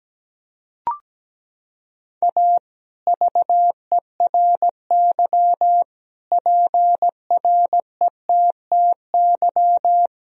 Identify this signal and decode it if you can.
Morse code, heard as a keyed tone, about 17 wpm, 700 Hz. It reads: A VERY PRETTY
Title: Sentences from Top 1000 Words - Review Part 2 - 17wpm